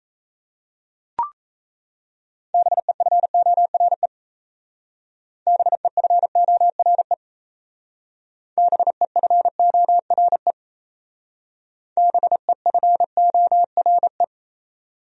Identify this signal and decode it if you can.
Morse code, heard as a keyed tone, about 35 wpm, 700 Hz. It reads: BEFORE BEFORE BEFORE BEFORE